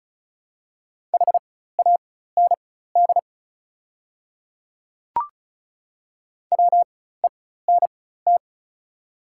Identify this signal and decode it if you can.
Morse code, heard as a keyed tone, about 36 wpm, spaced as for 16 wpm, 690 Hz. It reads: HAND WENT
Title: 4x Character Spacing - Top 300 Words - 35wpm